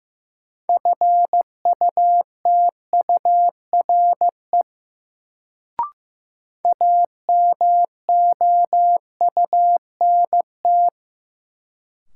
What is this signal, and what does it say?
Morse code, heard as a keyed tone, about 15 wpm, 700 Hz. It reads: FUTURE AMOUNT
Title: Sentences from Top 700 Words - New Word Drill - 15wpm